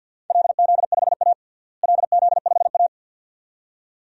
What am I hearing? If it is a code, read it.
Morse code, heard as a keyed tone, about 52 wpm, 700 Hz. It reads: LZ5A LZ5A
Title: DXCC-100 Call Signs - 4 Characters - Repeated 2x - 50wpm